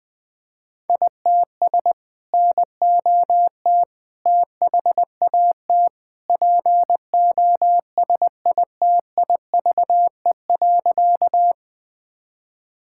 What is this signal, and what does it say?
Morse code, heard as a keyed tone, about 21 wpm, 700 Hz. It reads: ITS NOT THAT POSITIVE.
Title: Sentences from Top 1000 Words - Review Part 5 - 20wpm